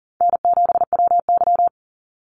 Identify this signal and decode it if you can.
Morse code, heard as a keyed tone, about 41 wpm, 700 Hz. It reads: N7WY
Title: Easy Call Signs - 40wpm